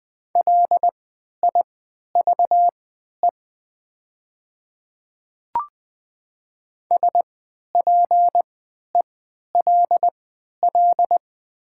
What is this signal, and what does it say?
Morse code, heard as a keyed tone, about 20 wpm, spaced as for 11 wpm, 700 Hz. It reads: LIVE SPELL
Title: 3x Character Spacing - Top 200 Words - 20wpm